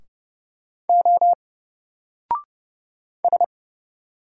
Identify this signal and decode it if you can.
Morse code, heard as a keyed tone, about 30 wpm, 700 Hz. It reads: O S
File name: Single-Letter-Number-30wpm